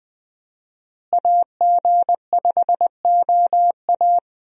Morse code, heard as a keyed tone, about 20 wpm, 700 Hz. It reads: AG5OA
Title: USA Call Signs - 5 Characters - Repeated 2x - 20wpm